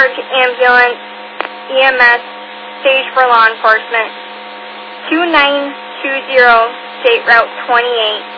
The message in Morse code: none

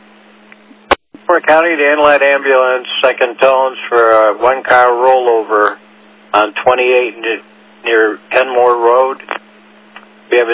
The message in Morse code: none

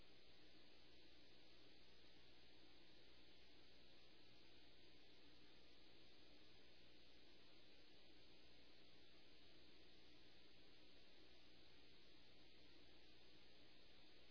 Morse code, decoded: none

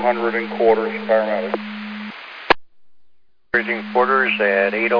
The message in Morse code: none

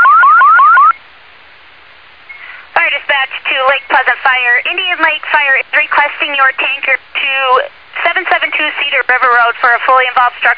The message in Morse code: none

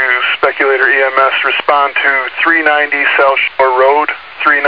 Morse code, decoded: none

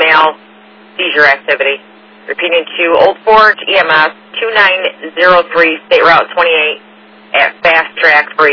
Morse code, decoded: none

